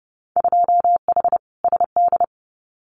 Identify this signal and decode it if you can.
Morse code, heard as a keyed tone, about 30 wpm, 700 Hz. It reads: 2H SD
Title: Winter Field Day Exchanges 30wpm